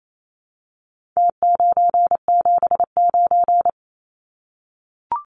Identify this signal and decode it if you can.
Morse code, heard as a keyed tone, about 28 wpm, 700 Hz. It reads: T979
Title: Numbers - Contest Style - 4-Digit 28wpm